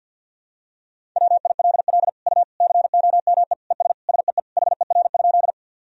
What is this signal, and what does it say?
Morse code, heard as a keyed tone, about 51 wpm, 700 Hz. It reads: WILL U COME ES SEE HER?
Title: Sentences from Top 100 Words and Abbreviations 50wpm